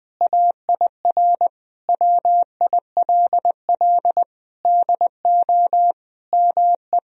Morse code, heard as a keyed tone, about 21 wpm, 700 Hz. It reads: AIR WILL DO ME
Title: Sentences from Top 1000 Words - Review Part 1 - 20wpm